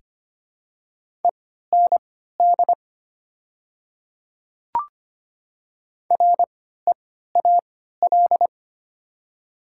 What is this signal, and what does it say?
Morse code, heard as a keyed tone, about 25 wpm, spaced as for 14 wpm, 700 Hz. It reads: END REAL
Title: 3x Character Spacing - Top 300 Words - 25wpm